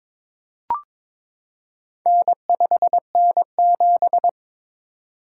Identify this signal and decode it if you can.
Morse code, heard as a keyed tone, about 22 wpm, 700 Hz. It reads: N5N7
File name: practice / 22wpm / Four Letter-Number Combinations - 22wpm